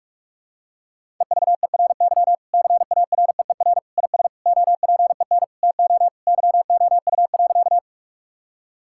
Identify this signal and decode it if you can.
Morse code, heard as a keyed tone, about 41 wpm, 700 Hz. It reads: EVERY CAREER IS OPEN TO YOU.
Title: Sentences from Top 1000 Words - Review Part 6 - 45wpm